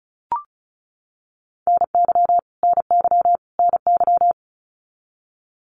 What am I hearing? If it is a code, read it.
Morse code, heard as a keyed tone, about 34 wpm, 700 Hz. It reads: NY NY NY
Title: US States Repeated 3x 35wpm